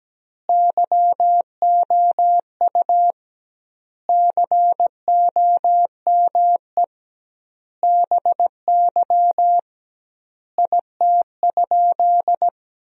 Morse code, heard as a keyed tone, about 17 wpm, 700 Hz. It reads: YOU COME BY IT?
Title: 2x Word-Spacing - Sentences from Top 100 Words - 17wpm